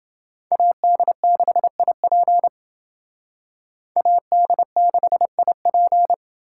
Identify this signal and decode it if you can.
Morse code, heard as a keyed tone, about 28 wpm, 700 Hz. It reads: AD6IP AD6IP